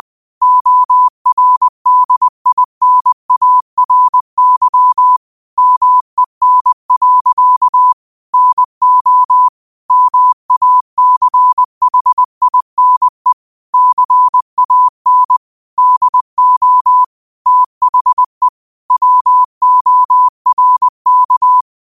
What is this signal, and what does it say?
Morse code, heard as a keyed tone, about 20 wpm, 1000 Hz. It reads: ORDINARY MEN. NO MACHINE CAN DO THE WORK
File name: qotdfiles / 20wpm